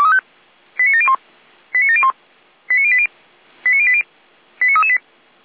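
Five-tone selective call (ZVEI), heard as quick runs of five tones, tones of about 70 ms. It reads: 89701 89701 89080 89080 89338
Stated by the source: ZVEI five-tone selective call 89700 received by SDR.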